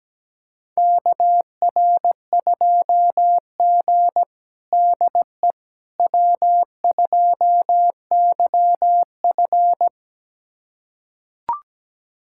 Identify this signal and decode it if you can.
Morse code, heard as a keyed tone, about 17 wpm, 700 Hz. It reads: KR2G DE W2YF
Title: USA Callsign DE Callsign - 4 Characters - 17wpm